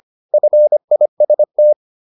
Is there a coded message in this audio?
Yes: Morse code from 0.3 to 1.7 s, heard as a keyed tone, about 25 wpm, 600 Hz. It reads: FIST